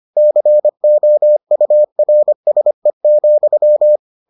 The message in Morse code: COURSE,

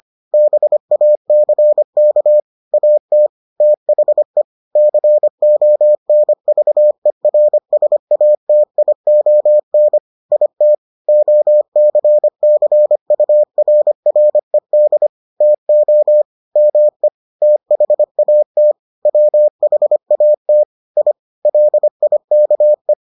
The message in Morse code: BACK AT THE CONVERSATION IT OCCURRED TO ME THAT WHAT I LIKE

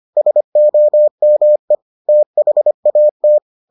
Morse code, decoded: SOME THAT

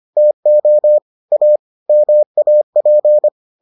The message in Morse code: TO A MAP

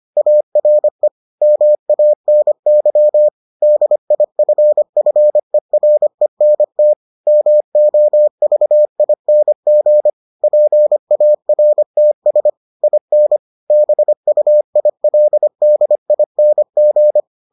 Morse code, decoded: ARE MANY DIFFERENT MOVING PARTS IN BUILDING

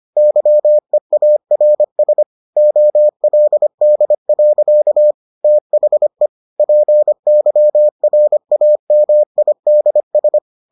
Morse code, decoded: YEARS OLD. THE PYRAMIDS

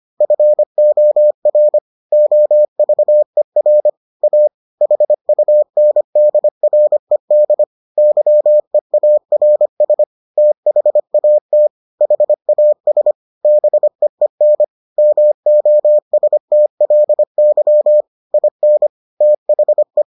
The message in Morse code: FOR OVER A HUNDRED YEARS THAT HAS BEEN MOSTLY IN THE

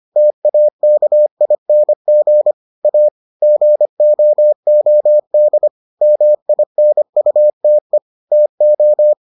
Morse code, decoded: TAKING A GOOD MINUTE TO

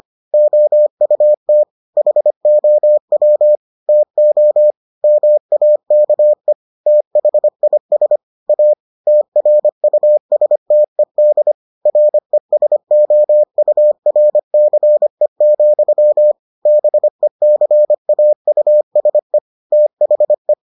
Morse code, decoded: OUT HOW TO MAKE THIS A TRUSTED RESOURCE, BECAUSE THE

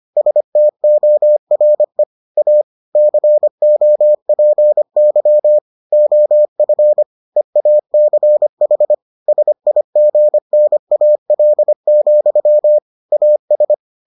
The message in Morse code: STORE A COPY OF EACH SIGNAL, AS

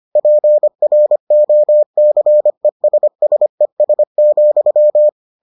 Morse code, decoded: PROCESSES,

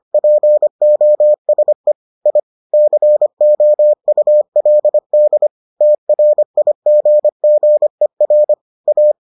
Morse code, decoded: POSE I COULD TRIGGER A

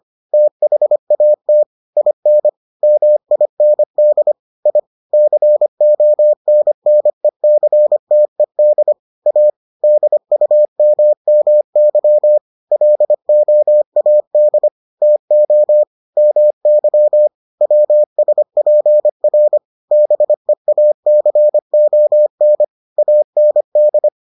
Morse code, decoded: THAT IN MIND I CONNECTED A DUMMY LOAD TO MY WSPR BEACON AND

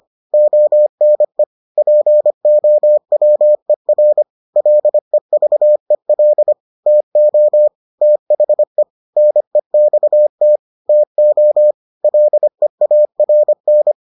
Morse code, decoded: ONE POWER LEVEL TO THE NEXT TO LEARN

